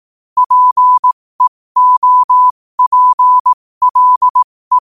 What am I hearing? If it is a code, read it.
Morse code, heard as a keyed tone, about 18 wpm, spaced as for 15 wpm, 1000 Hz. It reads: PEOPLE
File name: qotdfiles / 15wpm